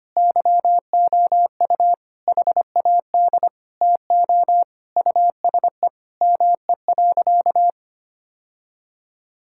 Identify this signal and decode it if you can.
Morse code, heard as a keyed tone, about 25 wpm, 700 Hz. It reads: YOU HAD TO USE ME.
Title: Sentences from Top 1000 Words - Review Part 6 - 25wpm